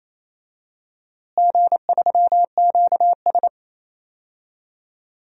Morse code, heard as a keyed tone, about 28 wpm, 700 Hz. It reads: G3QS
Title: DXCC-100 Call Signs - 4 Characters - Repeated 2x - 28wpm